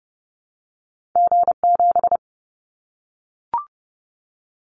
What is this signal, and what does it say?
Morse code, heard as a keyed tone, about 30 wpm, 700 Hz. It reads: G7
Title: Callsign Prefixes for Top 25 DXCC - 30wpm